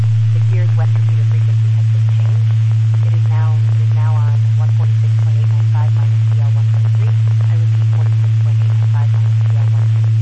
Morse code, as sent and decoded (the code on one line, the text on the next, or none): none